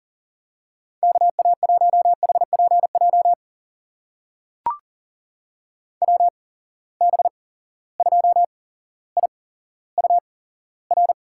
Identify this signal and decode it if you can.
Morse code, heard as a keyed tone, about 38 wpm, spaced as for 17 wpm, 700 Hz. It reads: KA1HPJ WB2IUR
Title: USA Call Signs - 6 Characters - 8x Character Spacing - 40wpm